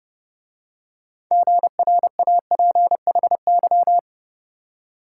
Morse code, heard as a keyed tone, about 29 wpm, 700 Hz. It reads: GRAPHY